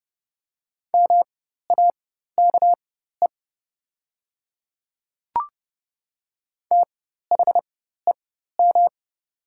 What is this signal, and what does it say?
Morse code, heard as a keyed tone, about 30 wpm, spaced as for 14 wpm, 700 Hz. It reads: MAKE THEM
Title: 4x Character Spacing - Top 100 Words - 30wpm